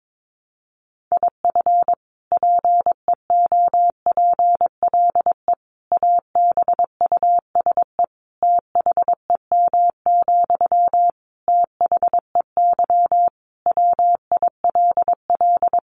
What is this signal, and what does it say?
Morse code, heard as a keyed tone, about 22 wpm, 700 Hz. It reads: IF PEOPLE ABUSE THEM, THEY WILL
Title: Sentences from Top 1300 Words 22wpm